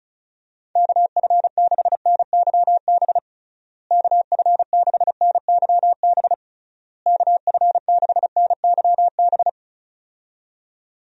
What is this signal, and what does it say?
Morse code, heard as a keyed tone, about 34 wpm, 700 Hz. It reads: KF6NYB KF6NYB KF6NYB
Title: USA Call Signs - 6 Characters - Repeated 3x - 35wpm